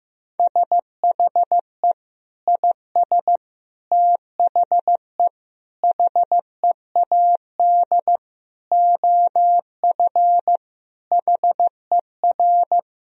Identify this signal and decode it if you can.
Morse code, heard as a keyed tone, about 15 wpm, 700 Hz. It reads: SHE IS THE HEAD OF HER
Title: Sentences from Top 1000 Words - Review Part 2 - 15wpm